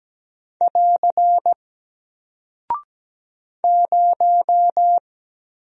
Morse code, heard as a keyed tone, about 17 wpm, 700 Hz. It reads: <AR> 0
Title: Single Character (Letter:Number:Punctuation:Prosign) Alt 17wpm